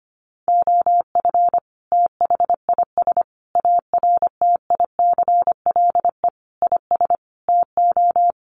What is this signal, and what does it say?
Morse code, heard as a keyed tone, about 25 wpm, 700 Hz. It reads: OF THIS ARTICLE IS TO
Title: Sets of 5 Words Encore2 25wpm